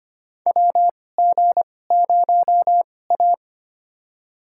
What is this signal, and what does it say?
Morse code, heard as a keyed tone, about 25 wpm, spaced as for 18 wpm, 700 Hz. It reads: WG0A
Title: USA Call Signs - 2x Character Spacing - 25wpm